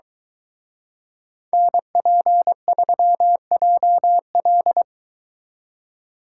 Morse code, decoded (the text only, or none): NP3JL